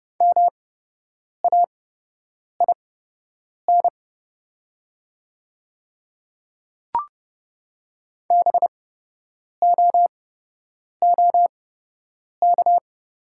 Morse code, heard as a keyed tone, about 30 wpm, spaced as for 8 wpm, 690 Hz. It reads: MAIN BOOK
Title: 8x Character Spacing - Top 300 Words - 30wpm